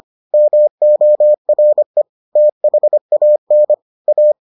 Morse code, heard as a keyed tone, about 25 wpm, 600 Hz. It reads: MORE THAN A